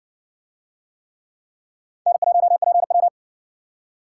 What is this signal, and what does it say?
Morse code, heard as a keyed tone, about 60 wpm, 700 Hz. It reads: N1PW